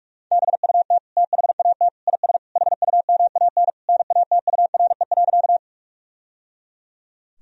Morse code, heard as a keyed tone, about 42 wpm, 700 Hz. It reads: BUT THAT IS HUMAN NATURE.
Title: Sentences from Top 1000 Words - Review Part 3 - 45wpm